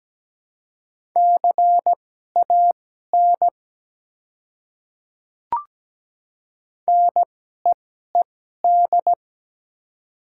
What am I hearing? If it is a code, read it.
Morse code, heard as a keyed tone, about 17 wpm, spaced as for 12 wpm, 700 Hz. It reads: CAN NEED
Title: 2x Character Spacing - Top 200 Words - 17wpm